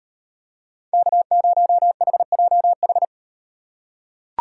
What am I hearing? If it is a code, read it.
Morse code, heard as a keyed tone, about 38 wpm, 700 Hz. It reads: K0HJH